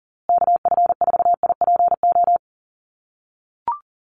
Morse code, heard as a keyed tone, about 40 wpm, 700 Hz. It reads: KF4IPO
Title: USA Call Signs Repeated 2x 40wpm